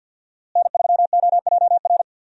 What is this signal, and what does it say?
Morse code, heard as a keyed tone, about 50 wpm, 700 Hz. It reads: N3OJR
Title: USA Call Signs - 5 Characters - Repeated 2x - 50wpm